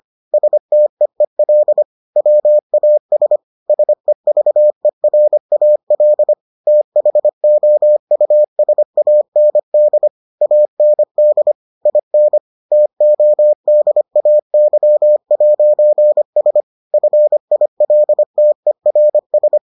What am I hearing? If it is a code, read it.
Morse code, heard as a keyed tone, about 25 wpm, 600 Hz. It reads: STEEL WAS SEVERAL THOUSAND AND IN TODAY'S FILTERS